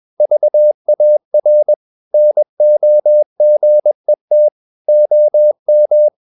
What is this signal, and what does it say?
Morse code, heard as a keyed tone, about 21 wpm, 600 Hz. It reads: VAR NOGET OM
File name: Ferie QRS 21wpm